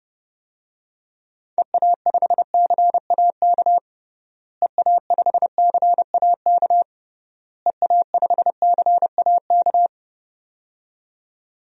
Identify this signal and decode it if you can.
Morse code, heard as a keyed tone, about 30 wpm, 700 Hz. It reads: EA5CAK EA5CAK EA5CAK
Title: DXCC-100 Call Signs - 6 Characters - Repeated 3x - 30wpm